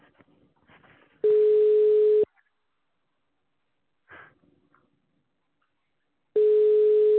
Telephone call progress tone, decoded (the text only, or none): ringback tone